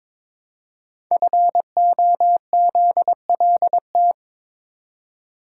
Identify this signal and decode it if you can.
Morse code, heard as a keyed tone, about 22 wpm, 700 Hz. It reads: FOZLT
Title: Code Groups - 5 Letters - 22wpm